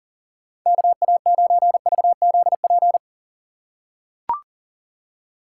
Morse code, heard as a keyed tone, about 37 wpm, 700 Hz. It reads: KA9VZP
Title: USA Call Signs - 6 Characters - 4x Character Spacing - 40wpm